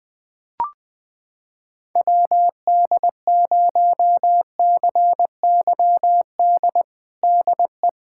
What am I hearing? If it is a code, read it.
Morse code, heard as a keyed tone, about 20 wpm, 700 Hz. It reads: WD0CYD DE
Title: USA Callsign DE Callsign - 6 Characters - 20wpm